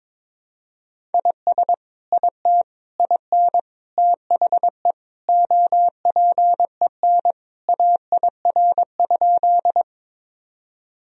Morse code, heard as a keyed tone, about 22 wpm, 700 Hz. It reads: IS IT IN THE OPEN AIR?
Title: Sentences from Top 1000 Words - Review Part 1 - 22wpm